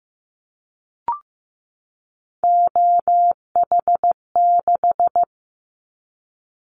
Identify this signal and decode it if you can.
Morse code, heard as a keyed tone, about 15 wpm, 700 Hz. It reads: OH6
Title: Callsign Prefixes for Top 50 DXCC - 15wpm